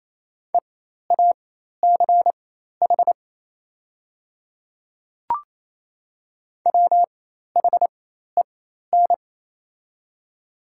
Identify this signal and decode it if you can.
Morse code, heard as a keyed tone, about 28 wpm, spaced as for 13 wpm, 700 Hz. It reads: EACH WHEN
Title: 4x Character Spacing - Top 100 Words - 28wpm